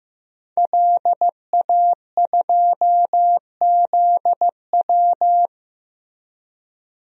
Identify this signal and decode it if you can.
Morse code, heard as a keyed tone, about 15 wpm, 700 Hz. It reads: LA2ZW
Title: DXCC-100 Call Signs - 5 Characters - Repeated 2x - 15wpm